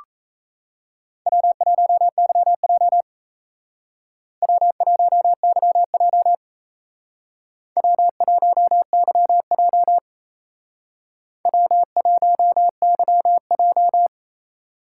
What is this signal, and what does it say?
Morse code, heard as a keyed tone, about 35 wpm, 700 Hz. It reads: W1YJ W1YJ W1YJ W1YJ